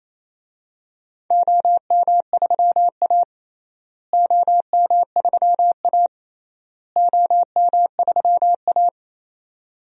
Morse code, heard as a keyed tone, about 28 wpm, 700 Hz. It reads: OM3A OM3A OM3A
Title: DXCC-100 Call Signs - 4 Characters - Repeated 3x - 28wpm